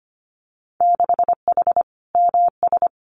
Morse code, heard as a keyed tone, about 25 wpm, 700 Hz. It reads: 6H MS
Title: Winter Field Day Exchanges Encore 25wpm